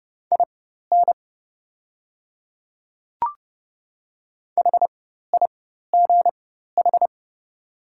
Morse code, heard as a keyed tone, about 30 wpm, spaced as for 14 wpm, 700 Hz. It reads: IN HIGH
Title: 4x Character Spacing - Top 200 Words - 30wpm